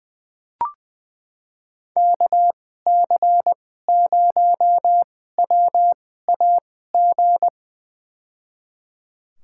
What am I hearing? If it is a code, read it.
Morse code, heard as a keyed tone, about 20 wpm, spaced as for 15 wpm, 700 Hz. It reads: KC0WAG
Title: USA Call Signs - 6 Characters - 2x Character Spacing - 20wpm